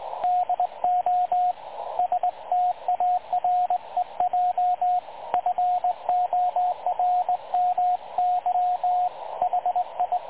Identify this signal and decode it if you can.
Morse code, heard as a keyed tone, about 21 wpm, 720 Hz, with radio noise. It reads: DO STAREJ FORMY HI
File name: dlazaawansowanych / nagraniazpasma / websdr recording 3565.0kHz